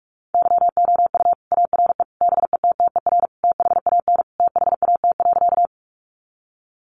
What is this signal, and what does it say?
Morse code, heard as a keyed tone, about 44 wpm, 700 Hz. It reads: YOU ARE BETTER THAN THAT.
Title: Sentences from Top 400 Words 45wpm